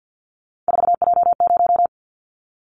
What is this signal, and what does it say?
Morse code, heard as a keyed tone, about 50 wpm, 700 Hz. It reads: 4J0
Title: Callsign Prefixes for Top 100 DXCC - 50wpm